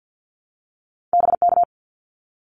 Morse code, heard as a keyed tone, about 50 wpm, 700 Hz. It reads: BK
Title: Abbreviations 50wpm